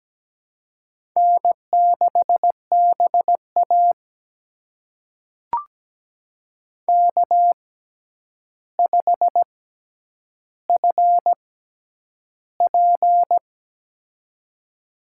Morse code, heard as a keyed tone, about 17 wpm, spaced as for 9 wpm, 700 Hz. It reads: N6BA K5FP
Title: USA Call Signs - 4 Characters - 6x Character Spacing - 17wpm